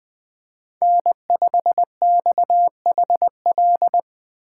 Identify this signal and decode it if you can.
Morse code, heard as a keyed tone, about 20 wpm, 700 Hz. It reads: N5XHL
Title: USA Call Signs - 5 Characters - 2x Character Spacing - 20wpm